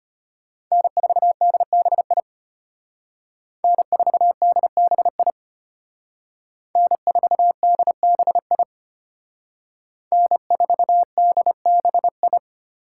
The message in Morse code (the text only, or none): N4DBI N4DBI N4DBI N4DBI